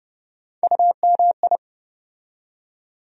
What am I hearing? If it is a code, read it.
Morse code, heard as a keyed tone, about 30 wpm, 700 Hz. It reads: UMI
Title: USA Call Sign Postfixes - 3 Letters 30wpm